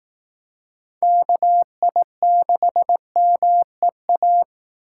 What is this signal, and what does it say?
Morse code, heard as a keyed tone, about 18 wpm, 700 Hz. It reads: KI6MEA
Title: Speed Racing - USA 2x3 Call Signs 15wpm